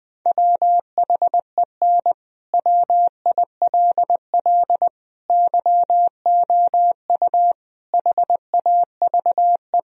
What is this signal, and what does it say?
Morse code, heard as a keyed tone, about 20 wpm, 700 Hz. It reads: WHEN WILL YOU HAVE
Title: Sentences from Top 1000 Words - Review Part 3 - 20wpm